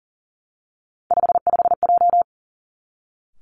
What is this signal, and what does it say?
Morse code, heard as a keyed tone, about 42 wpm, 690 Hz. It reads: 55J
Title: Three Challenging Letter-Number Combinations 40wpm